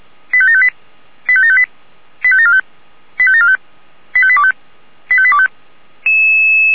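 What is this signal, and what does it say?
Five-tone selective call (ZVEI), heard as quick runs of five tones, tones of about 70 ms. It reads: 85758 85758 85745 85745 85825 85825
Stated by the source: ZVEI five-tone call received by SDR.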